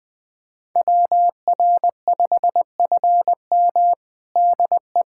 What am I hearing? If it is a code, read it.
Morse code, heard as a keyed tone, about 21 wpm, 700 Hz. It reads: WR5FM DE
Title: USA Callsign DE Callsign 20wpm